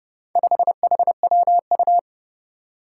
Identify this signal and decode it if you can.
Morse code, heard as a keyed tone, about 30 wpm, 700 Hz. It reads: 5HWU
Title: Four Challenging Letter-Number Combinations 30wpm